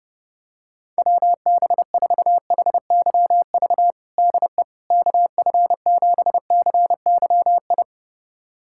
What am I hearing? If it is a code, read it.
Morse code, heard as a keyed tone, about 29 wpm, 700 Hz. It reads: WB4HYV DE KF7CYI
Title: USA Callsign DE Callsign - 6 Characters - 30wpm